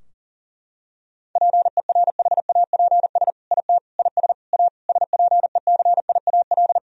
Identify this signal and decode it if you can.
Morse code, heard as a keyed tone, about 40 wpm, 700 Hz. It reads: PERHAPS IT IS A SPECIAL
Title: Sentences from Top 1000 Words - Review Part 3 - 40wpm